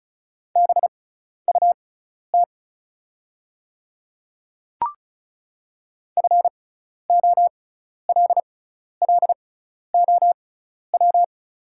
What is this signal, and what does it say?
Morse code, heard as a keyed tone, about 36 wpm, spaced as for 12 wpm, 690 Hz. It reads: BUT FOLLOW
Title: 6x Character Spacing - Top 300 Words - 35wpm